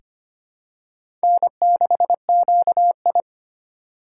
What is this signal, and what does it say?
Morse code, heard as a keyed tone, about 25 wpm, 700 Hz. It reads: N6QI
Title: USA Call Signs - 4 Characters - 4x Character Spacing - 25wpm